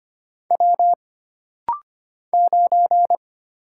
Morse code, heard as a keyed tone, about 25 wpm, 700 Hz. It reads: W 9
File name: Single Character (Letter:Number:Punctuation:Prosign) - Rapid-Fire - 25wpm